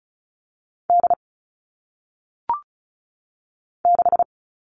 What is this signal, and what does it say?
Morse code, heard as a keyed tone, about 36 wpm, 700 Hz. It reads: D 6
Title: All Cut Numbers - 1-Digit 35wpm